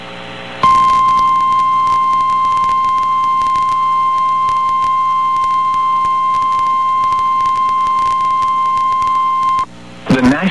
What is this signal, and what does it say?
Emergency Alert System (SAME) tones (the 1050 Hz attention tone), with speech. It is heard from 0.6 to 9.7 s.
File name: eas received alerts / L1 19543 audio